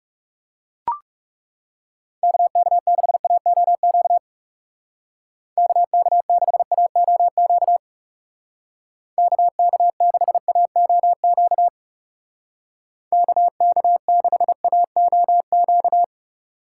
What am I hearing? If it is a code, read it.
Morse code, heard as a keyed tone, about 37 wpm, 700 Hz. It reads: KK6AOQ KK6AOQ KK6AOQ KK6AOQ
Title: Speed Racing - USA 2x3 Call Signs 30wpm